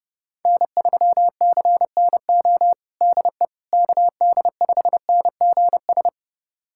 Morse code, heard as a keyed tone, about 32 wpm, 700 Hz. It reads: N3CNO DE KD5NGS
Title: USA Callsign DE Callsign 30wpm